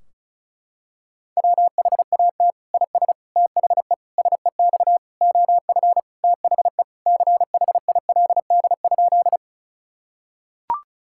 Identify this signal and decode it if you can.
Morse code, heard as a keyed tone, about 36 wpm, 700 Hz. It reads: WHAT IS THE SEX OF THE CHILD?
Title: Sentences from Top 1600 Words 35wpm